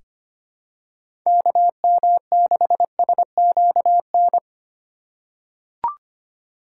Morse code, heard as a keyed tone, about 25 wpm, 700 Hz. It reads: KM6SQN
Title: USA Call Signs - 6 Characters - 2x Character Spacing - 25wpm